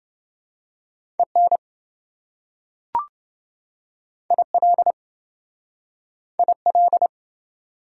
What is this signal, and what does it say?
Morse code, heard as a keyed tone, about 27 wpm, 700 Hz. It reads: EN IL IL